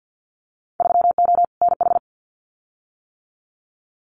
Morse code, heard as a keyed tone, about 50 wpm, 700 Hz. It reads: 3O NH
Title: Winter Field Day Exchanges Encore 50wpm